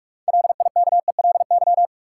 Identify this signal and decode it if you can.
Morse code, heard as a keyed tone, about 46 wpm, 700 Hz. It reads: LIKELY
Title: Sentences from Top 600 Words - New Word Drill - 45wpm